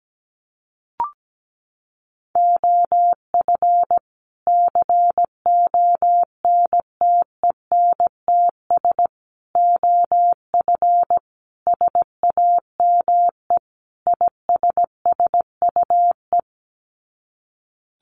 Morse code, heard as a keyed tone, about 17 wpm, 700 Hz. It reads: OF CONTENTS OF SAME ISSUE
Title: Sets of 5 Words Encore2 17wpm